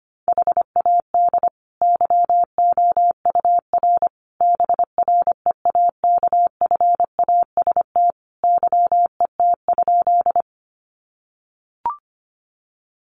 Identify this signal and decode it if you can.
Morse code, heard as a keyed tone, about 25 wpm, 700 Hz. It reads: HAD YOUR BREAKFAST YET?
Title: Sentences from Top 1500 Words 25wpm